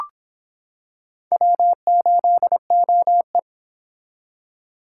Morse code, heard as a keyed tone, about 25 wpm, 700 Hz. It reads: W8OE